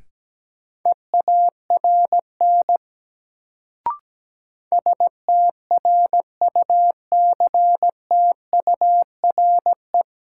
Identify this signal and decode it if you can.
Morse code, heard as a keyed tone, about 17 wpm, 700 Hz. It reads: EARN STRUCTURE
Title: Sentences from Top 1500 Words - New Word Drill Alt - 17wpm